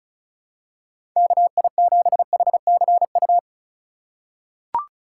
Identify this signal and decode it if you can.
Morse code, heard as a keyed tone, about 36 wpm, 700 Hz. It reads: KI7HCU
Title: USA Call Signs - 6 Characters - 4x Character Spacing - 35wpm